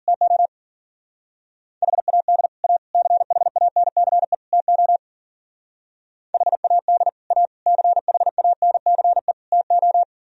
Morse code, heard as a keyed tone, about 50 wpm, 700 Hz. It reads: TO HAD A CHANCE TO HAD A CHANCE TO